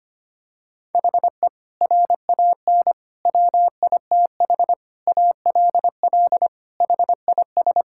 Morse code, heard as a keyed tone, about 25 wpm, 700 Hz. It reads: HE RAN WITH ALL HIS
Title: Sentences from Top 1000 Words - Review Part 5 - 25wpm